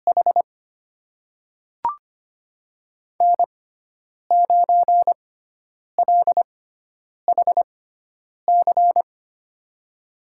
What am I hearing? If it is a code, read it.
Morse code, heard as a keyed tone, about 25 wpm, spaced as for 9 wpm, 700 Hz. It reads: H N9LHC
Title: USA Call Signs - 6x Character Spacing - 25wpm